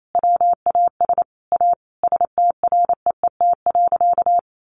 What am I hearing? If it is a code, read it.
Morse code, heard as a keyed tone, about 28 wpm, 700 Hz. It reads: WAS A STREET.
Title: Sentences from Top 1100 Words 28wpm